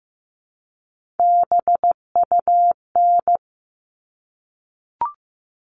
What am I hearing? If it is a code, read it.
Morse code, heard as a keyed tone, about 15 wpm, 700 Hz. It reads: BUN
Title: All Cut Numbers - 3-Digit 15wpm